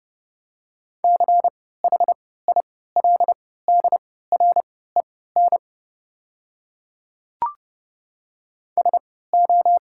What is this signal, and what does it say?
Morse code, heard as a keyed tone, about 30 wpm, spaced as for 17 wpm, 690 Hz. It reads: CHILDREN SO
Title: 3x Character Spacing - Top 500 Words - 30wpm